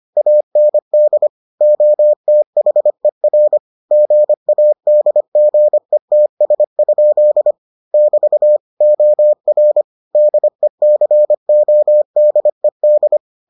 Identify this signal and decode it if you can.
Morse code, heard as a keyed tone, about 25 wpm, 600 Hz. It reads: AND OTHER GADGETS? <BT> OR DECODED